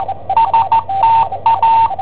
Morse code, heard as a keyed tone, about 14 wpm, 910 Hz, with radio noise. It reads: STA
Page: Cassetta esercizi preparata da I7OHP per IW5ELP